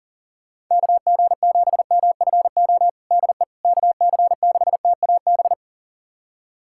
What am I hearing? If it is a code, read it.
Morse code, heard as a keyed tone, about 39 wpm, 700 Hz. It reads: KG7MFO DE KC6TAB
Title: USA Callsign DE Callsign 40wpm